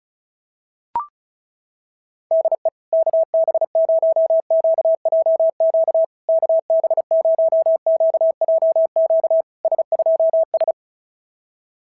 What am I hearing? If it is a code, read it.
Morse code, heard as a keyed tone, about 36 wpm, 660 Hz. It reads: DE KB0QJQ KB0QJQ S2S
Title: SOTA QSO Practice 35wpm